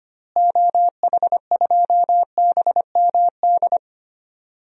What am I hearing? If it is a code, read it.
Morse code, heard as a keyed tone, about 25 wpm, 700 Hz. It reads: OH2BMD
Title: DXCC-100 Call Signs - 6 Characters - Repeated 3x - 25wpm